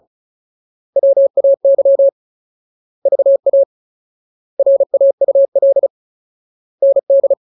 Morse code, heard as a keyed tone, about 35 wpm, 550 Hz. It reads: WAY VA RAUL ND
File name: Ultimate Word List / 35WPM Ultimate